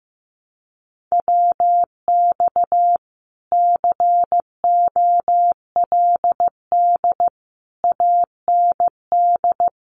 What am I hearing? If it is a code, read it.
Morse code, heard as a keyed tone, about 15 wpm, 700 Hz. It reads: WX COLD AND
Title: Common QSO Phrases Encore 15wpm